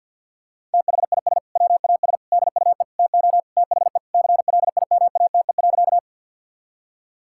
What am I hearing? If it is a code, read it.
Morse code, heard as a keyed tone, about 50 wpm, 700 Hz. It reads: THIS WAS DUE TO THE CLIMATE.
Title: Sentences from Top 1600 Words 50wpm